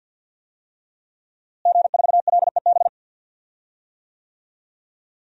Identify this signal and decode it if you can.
Morse code, heard as a keyed tone, about 50 wpm, 700 Hz. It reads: G4LEB
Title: DXCC-100 Call Signs - 5 Characters - Repeated 3x - 50wpm